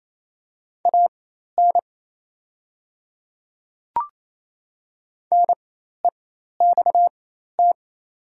Morse code, heard as a keyed tone, about 28 wpm, spaced as for 13 wpm, 700 Hz. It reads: AN NEXT
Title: 4x Character Spacing - Top 300 Words - 28wpm